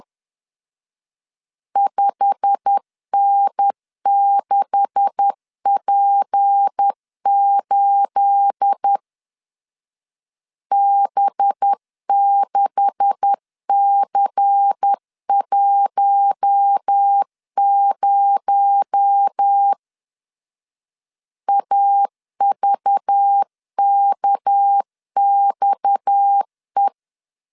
Morse code, heard as a keyed tone, about 10 wpm, 800 Hz. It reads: 5N6P8 B6C10 AVKXE
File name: CW / DET / 1 9 1500C S A 10wpm 4511